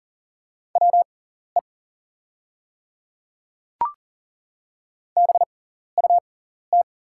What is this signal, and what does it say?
Morse code, heard as a keyed tone, about 40 wpm, spaced as for 14 wpm, 700 Hz. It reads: WE BUT
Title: 6x Character Spacing - Top 100 Words - 40wpm